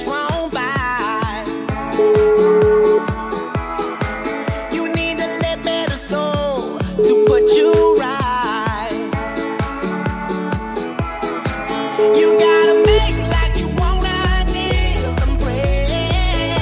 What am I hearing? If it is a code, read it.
Ringback tone (Europe, 425 Hz), with noise.